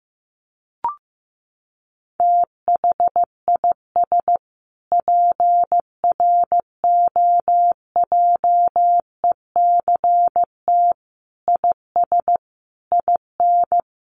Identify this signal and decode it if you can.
Morse code, heard as a keyed tone, about 15 wpm, 700 Hz. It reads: THIS PROJECT IS IN
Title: Sentences from Top 1400 Words 15wpm